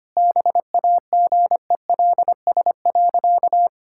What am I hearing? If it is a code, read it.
Morse code, heard as a keyed tone, about 25 wpm, 700 Hz. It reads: BAGELS.
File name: practice / 25wpm / All-Age Appropriate Jokes - 25wpm